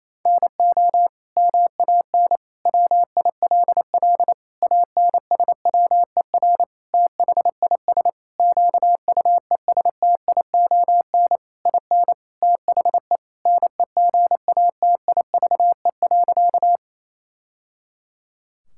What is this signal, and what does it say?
Morse code, heard as a keyed tone, about 28 wpm, 700 Hz. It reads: NO MAN WILL ANSWER THIS QUESTION IN THE NEGATIVE.
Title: Sentences from Top 1000 Words - Review Part 6 - 28wpm